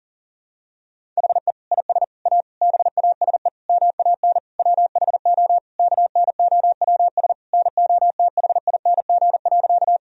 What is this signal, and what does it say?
Morse code, heard as a keyed tone, about 40 wpm, 700 Hz. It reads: HE IS A BASE MAN WHO KNOWS NOTHING.
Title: Sentences from Top 1000 Words - Review Part 1 - 40wpm